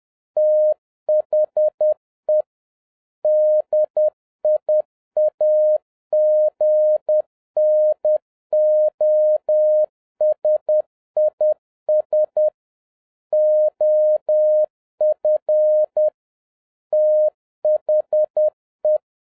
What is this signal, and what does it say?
Morse code, heard as a keyed tone, about 10 wpm, 610 Hz. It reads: THE DIAGNOSIS OF THE
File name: 2011-07-mp3 / n56 BACK FROM A BIG ACCIDENT 10wpm